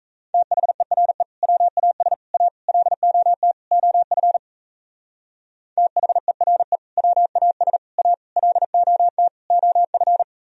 Morse code, heard as a keyed tone, about 37 wpm, 700 Hz. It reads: THERE WAS A LOT OF THERE WAS A LOT OF